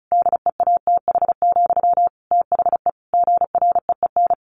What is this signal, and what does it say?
Morse code, heard as a keyed tone, about 34 wpm, 700 Hz. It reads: DEATH, THE GREEN
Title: A Sound Of Thunder 35wpm